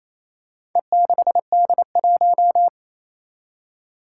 Morse code, heard as a keyed tone, about 27 wpm, 700 Hz. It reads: E6D1
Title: Four Challenging Letter-Number Combinations 28wpm